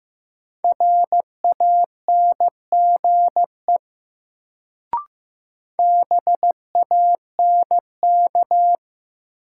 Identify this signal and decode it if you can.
Morse code, heard as a keyed tone, about 15 wpm, 700 Hz. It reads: RANGE BANK
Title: Sentences from Top 900 Words - New Word Drill - 15wpm